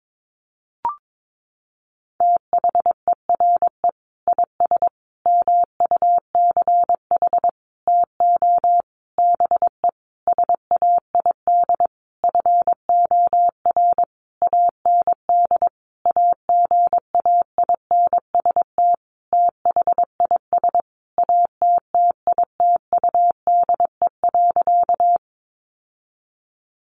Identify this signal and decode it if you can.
Morse code, heard as a keyed tone, about 22 wpm, 700 Hz. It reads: THERE IS MUCH TO BE SAID FOR AND AGAINST THIS ATTITUDE.
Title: Sentences from Top 1300 Words 22wpm